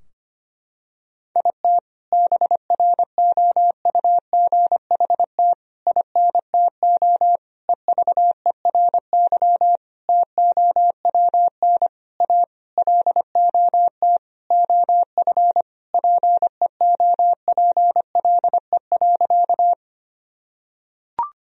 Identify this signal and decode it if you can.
Morse code, heard as a keyed tone, about 25 wpm, 700 Hz. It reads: IT BROUGHT INTO EVERY TOWN A LOT OF PEOPLE.
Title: Sentences from Top 1000 Words - Review Part 3 - 25wpm